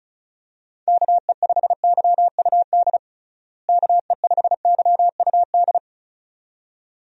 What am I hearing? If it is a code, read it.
Morse code, heard as a keyed tone, about 36 wpm, 700 Hz. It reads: KE5YUD KE5YUD
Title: USA Call Signs - 6 Characters - Repeated 2x - 35wpm